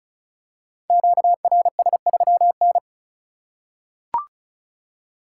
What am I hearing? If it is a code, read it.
Morse code, heard as a keyed tone, about 34 wpm, 700 Hz. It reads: QRS3N
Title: Code Groups - 5 Letter:Numbers - 35wpm